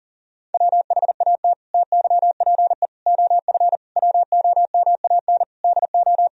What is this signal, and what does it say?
Morse code, heard as a keyed tone, about 37 wpm, 700 Hz. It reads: WHAT TYPE OF WOMAN DO
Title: Sentences from Top 1000 Words - Review Part 3 - 40wpm